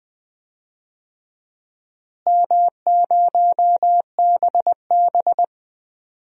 Morse code, heard as a keyed tone, about 20 wpm, 700 Hz. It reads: M0BB
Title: DXCC-100 Call Signs - 4 Characters - Repeated 3x - 20wpm